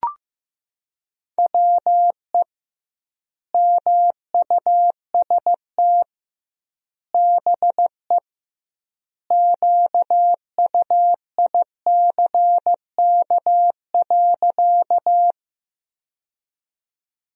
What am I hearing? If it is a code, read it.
Morse code, heard as a keyed tone, about 15 wpm, 700 Hz. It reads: WE MUST BE QUICK.
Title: 2x Word-Spacing - Sentences from Top 500 Words - 15wpm